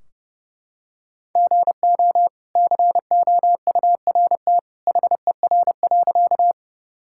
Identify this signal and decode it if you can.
Morse code, heard as a keyed tone, about 30 wpm, 700 Hz. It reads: GO COURT HER.
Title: Sentences from Top 1600 Words 30wpm